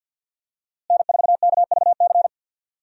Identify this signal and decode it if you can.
Morse code, heard as a keyed tone, about 52 wpm, 700 Hz. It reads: N4KVC
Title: USA Call Signs - 5 Characters - Repeated 3x - 50wpm